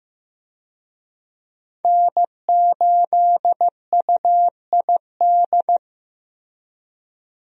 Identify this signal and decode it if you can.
Morse code, heard as a keyed tone, about 15 wpm, 700 Hz. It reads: N8UID
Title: USA Call Signs - 5 Characters - Repeated 2x - 15wpm